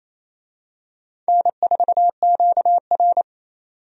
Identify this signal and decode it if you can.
Morse code, heard as a keyed tone, about 28 wpm, 700 Hz. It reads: N4QR
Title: USA Call Signs - 4 Characters - 28wpm